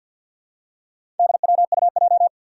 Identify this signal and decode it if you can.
Morse code, heard as a keyed tone, about 53 wpm, 700 Hz. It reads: DWUJ
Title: Four Challenging Letter-Number Combinations 50wpm